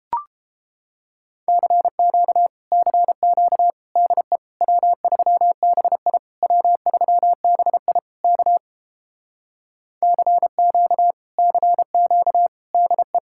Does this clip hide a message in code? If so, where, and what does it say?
Yes: Morse code from 1.5 to 13.2 s, heard as a keyed tone, about 29 wpm, 700 Hz. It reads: CQ CQ DE W3BI W3BI K CQ CQ DE